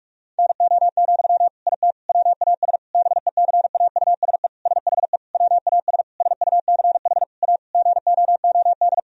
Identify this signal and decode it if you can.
Morse code, heard as a keyed tone, about 41 wpm, 700 Hz. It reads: NO, IT WAS BECAUSE SHE WAS SUCH A GOOD
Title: Sentences from Top 1000 Words - Review Part 4 - 45wpm